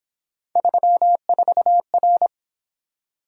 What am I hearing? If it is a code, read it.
Morse code, heard as a keyed tone, about 28 wpm, 700 Hz. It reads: 34R